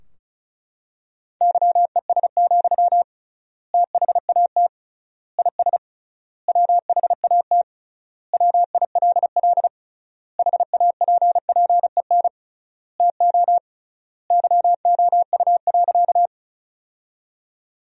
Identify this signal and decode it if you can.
Morse code, heard as a keyed tone, about 36 wpm, 700 Hz. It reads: YES, THAT IS WHAT WILL HAPPEN TO YOU.
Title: 3x Word-Spacing - Sentences from Top 500 Words - 35wpm